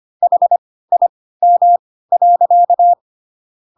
Morse code, heard as a keyed tone, about 25 wpm, spaced as for 16 wpm, 700 Hz. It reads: HIM.
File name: Pinocchio / Pinocchio ch3 10 25 ik1hge